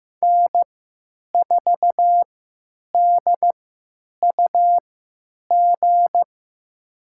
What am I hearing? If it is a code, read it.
Morse code, heard as a keyed tone, about 15 wpm, spaced as for 8 wpm, 700 Hz. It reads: N4DUG
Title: USA Call Signs - 5 Characters - 3x Character Spacing - 15wpm